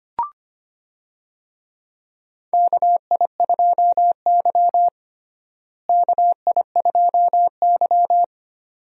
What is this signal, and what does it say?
Morse code, heard as a keyed tone, about 25 wpm, 700 Hz. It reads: KI2Y KI2Y
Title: USA Call Signs - 4 Characters - Repeated 2x - 25wpm